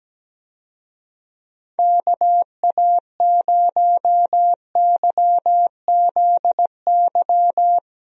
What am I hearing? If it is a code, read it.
Morse code, heard as a keyed tone, about 17 wpm, 700 Hz. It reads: KA0YZY
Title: USA Call Signs - 6 Characters - Repeated 3x - 17wpm